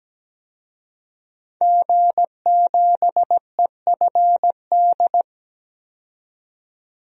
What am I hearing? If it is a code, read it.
Morse code, heard as a keyed tone, about 17 wpm, 700 Hz. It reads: G7EFD
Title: DXCC-100 Call Signs - 5 Characters - 17wpm